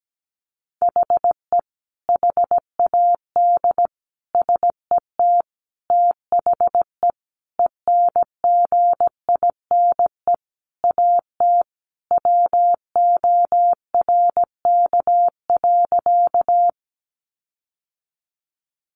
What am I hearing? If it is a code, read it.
Morse code, heard as a keyed tone, about 17 wpm, 700 Hz. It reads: HE HAD SET THE ENGINE AT WORK.
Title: Sentences from Top 1500 Words 17wpm